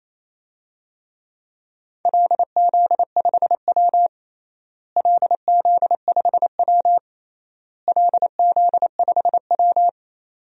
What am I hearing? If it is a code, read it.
Morse code, heard as a keyed tone, about 28 wpm, 700 Hz. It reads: LZ5W LZ5W LZ5W
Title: DXCC-100 Call Signs - 4 Characters - Repeated 3x - 28wpm